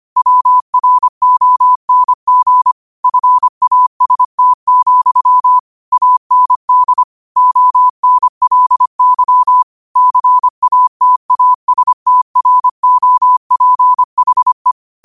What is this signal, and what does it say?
Morse code, heard as a keyed tone, about 25 wpm, 1000 Hz. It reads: WRONG FAST, AND ONLY CATASTROPHE
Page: New quotes every day in morse code at 25 Words per minute.